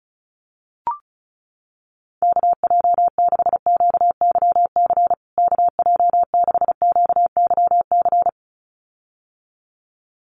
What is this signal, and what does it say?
Morse code, heard as a keyed tone, about 34 wpm, 700 Hz. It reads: KJ6QYC KJ6QYC
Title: USA Call Signs Repeated 2x 35wpm